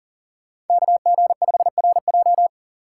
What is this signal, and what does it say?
Morse code, heard as a keyed tone, about 52 wpm, 700 Hz. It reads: KG5RJ